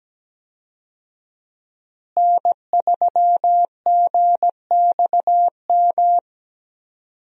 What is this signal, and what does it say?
Morse code, heard as a keyed tone, about 17 wpm, 700 Hz. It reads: N3GXM
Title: USA Call Signs - 5 Characters - Repeated 2x - 17wpm